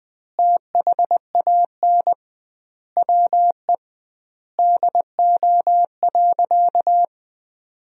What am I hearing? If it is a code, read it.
Morse code, heard as a keyed tone, about 20 wpm, 700 Hz. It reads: THAN WE DO.
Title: 2x Word-Spacing - Sentences from Top 100 Words - 20wpm